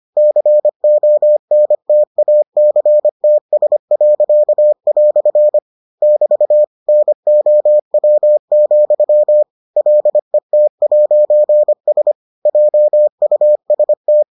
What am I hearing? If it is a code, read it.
Morse code, heard as a keyed tone, about 25 wpm, 600 Hz. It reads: CONTACTS." <BT> NOW, LET'S JUST